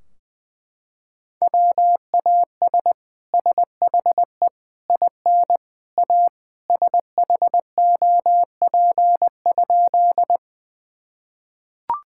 Morse code, heard as a keyed tone, about 21 wpm, 700 Hz. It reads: WAS SHE IN A SHOP?
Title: Sentences from Top 1600 Words 20wpm